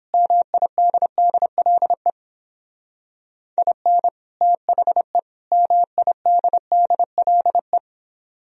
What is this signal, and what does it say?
Morse code, heard as a keyed tone, about 31 wpm, 700 Hz. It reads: MIDDLE IN THE MIDDLE